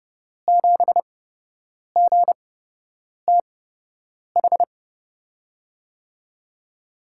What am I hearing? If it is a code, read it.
Morse code, heard as a keyed tone, about 30 wpm, spaced as for 8 wpm, 700 Hz. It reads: 7GTH
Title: USA Call Signs - 6 Characters - 8x Character Spacing - 30wpm